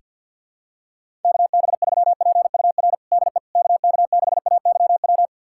Morse code, heard as a keyed tone, about 52 wpm, 700 Hz. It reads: KB3PUR DE KK6AYW
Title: USA Callsign DE Callsign - 6 Characters - 50wpm